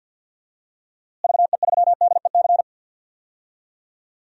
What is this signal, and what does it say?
Morse code, heard as a keyed tone, about 51 wpm, 700 Hz. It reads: VE3DEC
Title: DXCC-100 Call Signs - 6 Characters - Repeated 3x - 50wpm